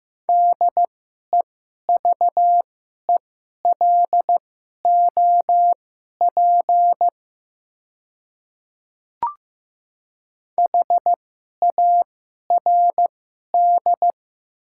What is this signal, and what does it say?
Morse code, heard as a keyed tone, about 15 wpm, spaced as for 11 wpm, 700 Hz. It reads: DEVELOP HARD
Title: 2x Character Spacing - Top 500 Words - 15wpm